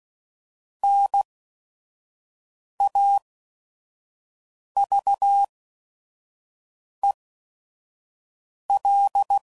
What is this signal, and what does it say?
Morse code, heard as a keyed tone, about 16 wpm, spaced as for 5 wpm, 800 Hz. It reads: NAVEL
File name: Prefixes / By First Letter / Ns